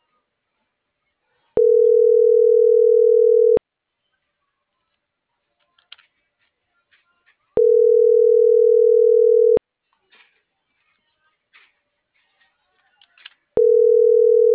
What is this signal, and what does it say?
Ringback tone (North America), 440 + 480 Hz.